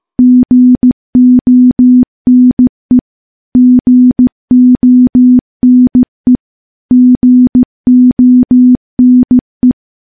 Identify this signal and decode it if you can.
Morse code, heard as a keyed tone, about 15 wpm, 250 Hz. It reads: GONE GONE GONE
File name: morse-training-15WPM